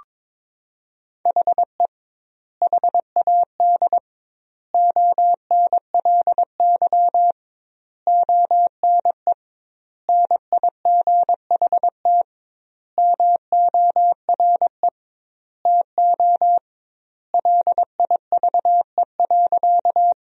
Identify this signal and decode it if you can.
Morse code, heard as a keyed tone, about 21 wpm, 700 Hz. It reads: HE HAD ONLY ONE NIGHT MORE TO LIVE.
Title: 2x Word-Spacing - Sentences from Top 300 Words - 22wpm